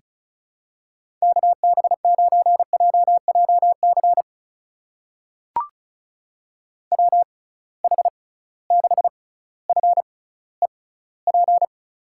Morse code, heard as a keyed tone, about 36 wpm, spaced as for 18 wpm, 700 Hz. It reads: KB9JJC WH6FEP
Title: USA Call Signs - 6x Character Spacing - 35wpm